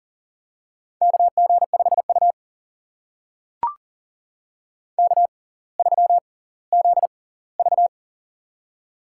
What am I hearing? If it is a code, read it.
Morse code, heard as a keyed tone, about 40 wpm, spaced as for 20 wpm, 700 Hz. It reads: KG5U K3ZV
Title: USA Call Signs - 6x Character Spacing - 40wpm